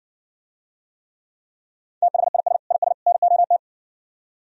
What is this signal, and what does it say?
Morse code, heard as a keyed tone, about 69 wpm, 700 Hz. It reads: THIS IS NOT